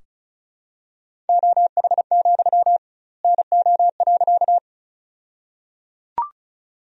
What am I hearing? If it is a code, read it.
Morse code, heard as a keyed tone, about 36 wpm, 700 Hz. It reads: OH, NO.
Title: 2x Word-Spacing - Sentences from Top 500 Words - 35wpm